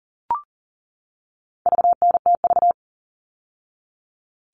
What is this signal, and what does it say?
Morse code, heard as a keyed tone, about 40 wpm, 700 Hz. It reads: VNTV
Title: All Cut Numbers - 4-Digit 40wpm